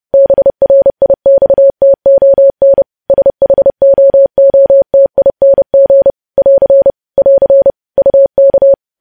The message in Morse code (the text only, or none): BRIXTON SHOOTING <AR> <AR> UK